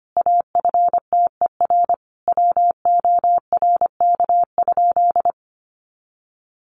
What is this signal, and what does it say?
Morse code, heard as a keyed tone, about 25 wpm, 700 Hz. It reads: AFTER WORK?
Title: Sentences from Top 1400 Words 25wpm